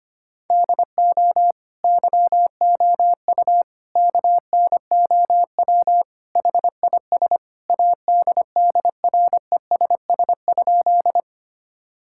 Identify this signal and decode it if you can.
Morse code, heard as a keyed tone, about 25 wpm, 700 Hz. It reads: DO YOU KNOW HIS ADDRESS?
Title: Sentences from Top 1000 Words - Review Part 6 - 25wpm